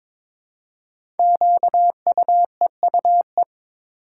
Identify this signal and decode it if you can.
Morse code, heard as a keyed tone, about 22 wpm, 700 Hz. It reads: QUEUE
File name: 5 Letter Words - First 3 Letters Given - 22wpm